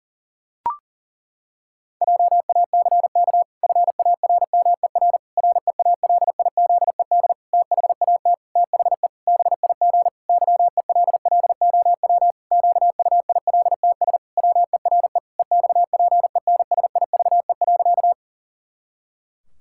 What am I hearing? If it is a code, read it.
Morse code, heard as a keyed tone, about 37 wpm, 700 Hz. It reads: JACK FARMER REALIZED THAT THE BIG YELLOW QUILTS WERE EXPENSIVE.
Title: Pangram Challenge v2 - 40wpm